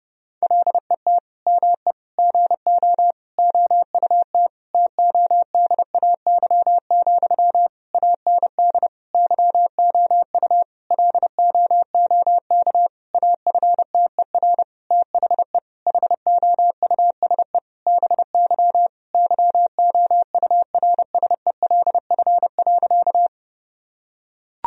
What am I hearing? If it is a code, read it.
Morse code, heard as a keyed tone, about 32 wpm, 700 Hz. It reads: LET ME GO OUT TODAY, AND YOU LOOK AFTER THE HOUSE BY YOURSELF.
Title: Sentences from Top 1000 Words - Review Part 4 - 30wpm